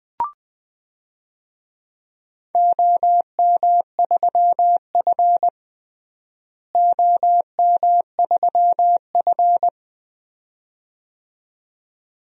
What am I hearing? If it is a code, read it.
Morse code, heard as a keyed tone, about 21 wpm, 700 Hz. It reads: OM3F OM3F
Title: DXCC-100 Call Signs - 4 Characters - Repeated 2x - 20wpm